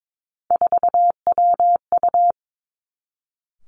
Morse code, heard as a keyed tone, about 22 wpm, 700 Hz. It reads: 4WU
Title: Three Challenging Letter-Number Combinations 22wpm